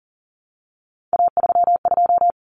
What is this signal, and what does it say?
Morse code, heard as a keyed tone, about 40 wpm, 700 Hz. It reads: A32
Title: Common Cut Numbers - 3-Digit 40wpm